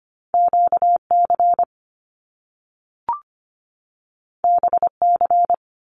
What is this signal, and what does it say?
Morse code, heard as a keyed tone, about 25 wpm, 700 Hz. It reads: QC BC
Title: Canadian Provinces and Territories 25wpm